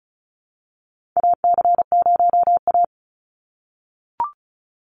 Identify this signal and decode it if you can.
Morse code, heard as a keyed tone, about 34 wpm, 700 Hz. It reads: AC0A
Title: NA SOTA Chaser Callsigns 35wpm